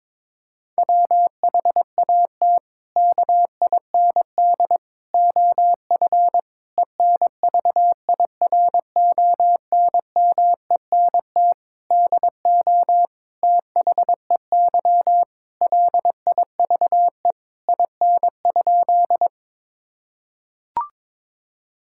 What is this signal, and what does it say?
Morse code, heard as a keyed tone, about 23 wpm, 700 Hz. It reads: WHAT KIND OF ENVIRONMENT DO THEY LIVE IN?
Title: Sentences from Top 1000 Words - Review Part 5 - 22wpm